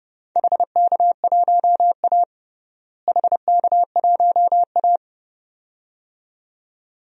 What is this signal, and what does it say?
Morse code, heard as a keyed tone, about 30 wpm, 700 Hz. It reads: HK1A HK1A
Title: DXCC-100 Call Signs - 4 Characters - Repeated 2x - 30wpm